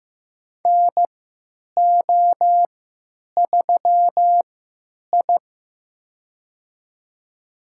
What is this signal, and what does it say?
Morse code, heard as a keyed tone, about 15 wpm, spaced as for 9 wpm, 700 Hz. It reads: NO3I
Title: USA Call Signs - 3x Character Spacing - 15wpm